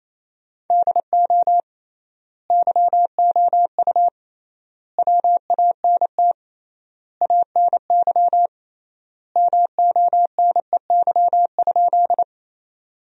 Morse code, heard as a keyed tone, about 28 wpm, 700 Hz. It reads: DO YOU WANT ANY MONEY?
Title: 3x Word-Spacing - Sentences from Top 400 Words - 28wpm